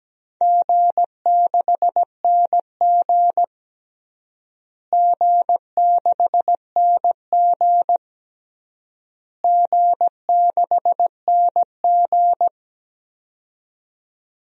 Morse code, heard as a keyed tone, about 17 wpm, 700 Hz. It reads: G6NG G6NG G6NG
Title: DXCC-100 Call Signs - 4 Characters - Repeated 3x - 17wpm